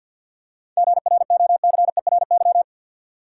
Morse code, heard as a keyed tone, about 51 wpm, 700 Hz. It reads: GROCERY
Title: Seven Letter Words - 50wpm